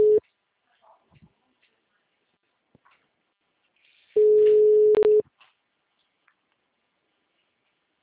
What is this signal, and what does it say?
Ringback tone (Europe, 425 Hz), with speech.